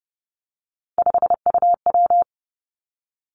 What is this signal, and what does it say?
Morse code, heard as a keyed tone, about 30 wpm, 700 Hz. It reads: 5UW
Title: Three Challenging Letter-Number Combinations 30wpm